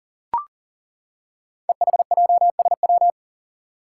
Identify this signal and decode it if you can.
Morse code, heard as a keyed tone, about 39 wpm, 690 Hz. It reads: EHJSW
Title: Code Groups - 5 Letters - 40wpm